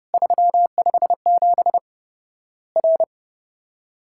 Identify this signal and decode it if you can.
Morse code, heard as a keyed tone, about 30 wpm, 700 Hz. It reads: 357 R
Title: Worked All Europe DX Contest - QTC Exchange - 30wpm